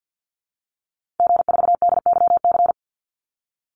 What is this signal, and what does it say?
Morse code, heard as a keyed tone, about 50 wpm, 700 Hz. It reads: G4DYC
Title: Call Signs - 50wpm